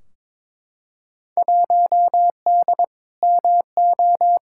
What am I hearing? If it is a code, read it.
Morse code, heard as a keyed tone, about 22 wpm, 700 Hz. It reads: 1D MO
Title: Field Day Exchanges - 22wpm